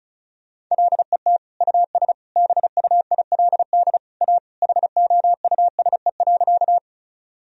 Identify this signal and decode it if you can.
Morse code, heard as a keyed tone, about 34 wpm, 700 Hz. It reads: LET US BUILD A HOUSE.
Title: Sentences from Top 1000 Words - Review Part 1 - 35wpm